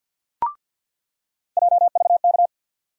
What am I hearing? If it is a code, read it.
Morse code, heard as a keyed tone, about 47 wpm, 700 Hz. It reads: JVK
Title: USA Call Sign Postfixes - 3 Letters 50wpm